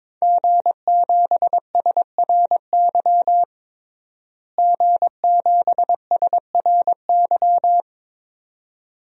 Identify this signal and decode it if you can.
Morse code, heard as a keyed tone, about 22 wpm, 700 Hz. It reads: G7SRY G7SRY
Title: DXCC-100 Call Signs - 5 Characters - Repeated 2x - 22wpm